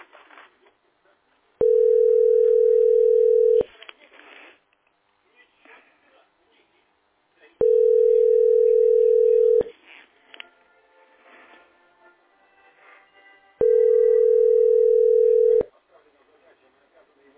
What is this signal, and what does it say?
Ringback tone (North America), 440 + 480 Hz, with speech.
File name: 7b61aab5-8eb5-4e36-9db0